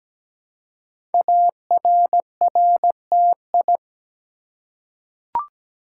Morse code, heard as a keyed tone, about 17 wpm, 700 Hz. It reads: ARRTI
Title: Code Groups - 5 Letters - 17wpm